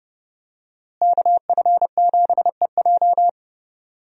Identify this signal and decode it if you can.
Morse code, heard as a keyed tone, about 32 wpm, 700 Hz. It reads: KF7EJ
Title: USA Call Signs - 5 Characters - Repeated 3x - 30wpm